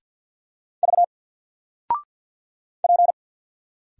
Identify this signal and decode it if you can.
Morse code, heard as a keyed tone, about 50 wpm, 700 Hz. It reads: V P
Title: Single Letter-Number Alt 50wpm